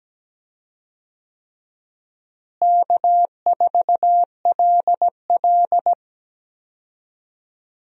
Morse code, heard as a keyed tone, about 17 wpm, 700 Hz. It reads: K4LL
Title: USA Call Signs - 4 Characters - Repeated 3x - 17wpm